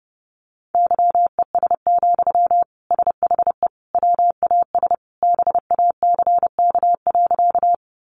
Morse code, heard as a keyed tone, about 30 wpm, 700 Hz. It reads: YES, SHE WAS BACK.
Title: Sentences from Top 500 Words 30wpm